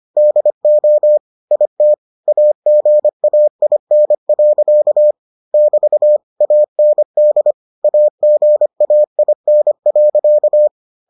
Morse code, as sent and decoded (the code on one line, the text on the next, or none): -.. --- / .. - / .- --. .- .. -. .-.-.- / -...- / .- -. -.. / .- --. .- .. -. .-.-.-
DO IT AGAIN. <BT> AND AGAIN.